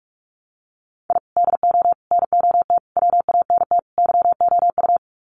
Morse code, heard as a keyed tone, about 44 wpm, 700 Hz. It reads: I DO NOT WANT YOU
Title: Sets of 5 Words 45wpm